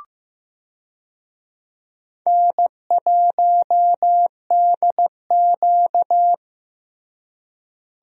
Morse code, heard as a keyed tone, about 15 wpm, 700 Hz. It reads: N1DQ
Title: USA Call Signs - 4 Characters - Repeated 2x - 15wpm